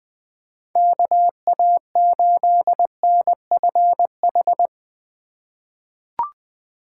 Morse code, heard as a keyed tone, about 20 wpm, 700 Hz. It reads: KA8NFH
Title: USA Call Signs - 4x Character Spacing - 20wpm